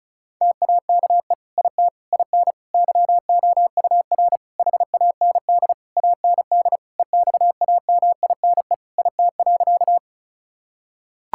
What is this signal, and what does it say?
Morse code, heard as a keyed tone, about 36 wpm, 700 Hz. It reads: TAKE IT IN YOUR HAND AND EXAMINE IT.
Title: Sentences from Top 1600 Words 35wpm